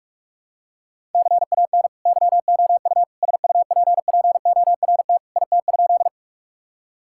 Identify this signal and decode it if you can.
Morse code, heard as a keyed tone, about 41 wpm, 700 Hz. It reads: CAN YOU SUPPORT IT?
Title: Sentences from Top 1000 Words - Review Part 5 - 45wpm